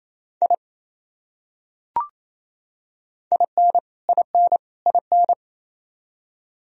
Morse code, heard as a keyed tone, about 28 wpm, 700 Hz. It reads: I IN IN IN
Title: Top 100 Words Repeated 28wpm